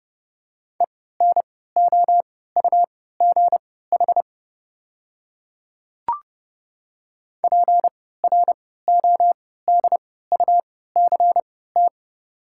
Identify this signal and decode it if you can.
Morse code, heard as a keyed tone, about 31 wpm, spaced as for 17 wpm, 700 Hz. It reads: ENOUGH PRODUCT
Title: 3x Character Spacing - Top 500 Words - 30wpm